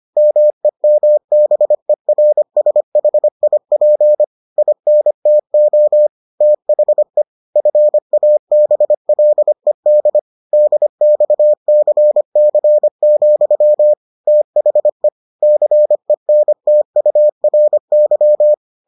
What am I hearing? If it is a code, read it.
Morse code, heard as a keyed tone, about 25 wpm, 600 Hz. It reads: MEMBERSHIP INTO THE FABLED DXCC, THE CENTURY